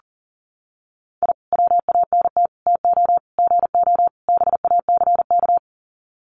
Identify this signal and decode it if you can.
Morse code, heard as a keyed tone, about 40 wpm, 700 Hz. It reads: I WANT TO GO BACK
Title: Sets of 5 Words Encore 40wpm